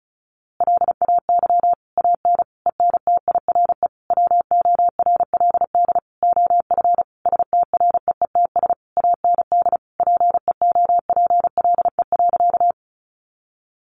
Morse code, heard as a keyed tone, about 34 wpm, 700 Hz. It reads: LAY AN ENTIRE WORLD OF STREETS AND PEOPLE.
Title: A Sound Of Thunder 35wpm